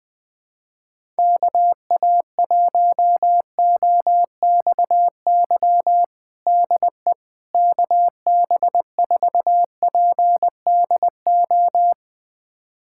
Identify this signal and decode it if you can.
Morse code, heard as a keyed tone, about 20 wpm, 700 Hz. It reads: KA1OXY DE KB4PDO
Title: USA Callsign DE Callsign - 6 Characters - 20wpm